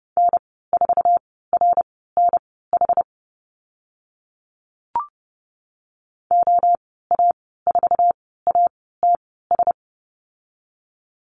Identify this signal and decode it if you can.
Morse code, heard as a keyed tone, about 30 wpm, spaced as for 17 wpm, 700 Hz. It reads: N4RNH OA4ATS
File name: Lesson 14g - Number 2 - Call Signs 30wpm